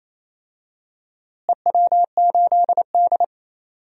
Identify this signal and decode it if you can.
Morse code, heard as a keyed tone, about 28 wpm, 700 Hz. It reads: EW8D
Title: DXCC-100 Call Signs - 4 Characters - Repeated 2x - 28wpm